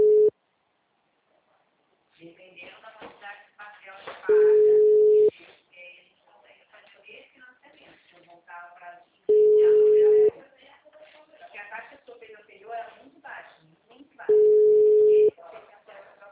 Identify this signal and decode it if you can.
Ringback tone (Europe, 425 Hz), with speech.